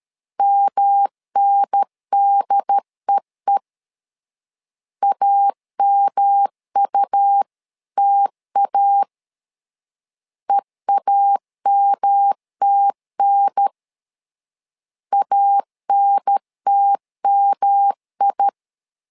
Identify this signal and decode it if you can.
Morse code, heard as a keyed tone, about 13 wpm, 800 Hz. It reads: MNDEE AMUTA EAMTN ANTMI
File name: LEZIONI / 3 MWRO / 3 M 12wpm 1541